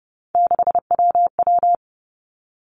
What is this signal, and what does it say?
Morse code, heard as a keyed tone, about 30 wpm, 700 Hz. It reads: 6WW
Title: Three Challenging Letter-Number Combinations 30wpm